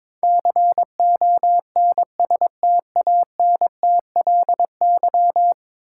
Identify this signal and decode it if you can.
Morse code, heard as a keyed tone, about 22 wpm, 700 Hz. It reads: CONSTANTLY
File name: practice / 22wpm / 10 Letter Words - First 3 Letters Given - 22wpm